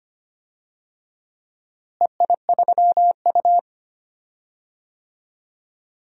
Morse code, heard as a keyed tone, about 25 wpm, 700 Hz. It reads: EI3U
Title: DXCC-100 Call Signs - 4 Characters - Repeated 3x - 25wpm